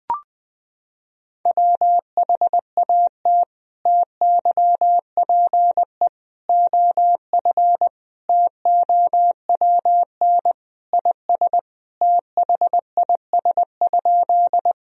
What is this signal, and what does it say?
Morse code, heard as a keyed tone, about 20 wpm, 700 Hz. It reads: WHAT TYPE OF TOWN IS THIS?
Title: Sentences from Top 1000 Words - Review Part 3 - 20wpm